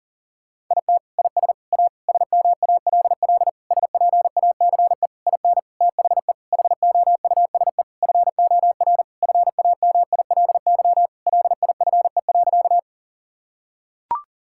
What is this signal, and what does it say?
Morse code, heard as a keyed tone, about 40 wpm, 700 Hz. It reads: IT IS A SMALL SPACE IN THE HOUSE FOR FAMILY LIFE.
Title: Sentences from Top 1000 Words - Review Part 5 - 40wpm